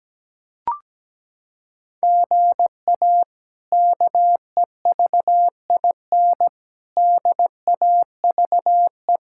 Morse code, heard as a keyed tone, about 17 wpm, 700 Hz. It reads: GA KEVIN DAVE
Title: K1USN SST QSO Practice - 17wpm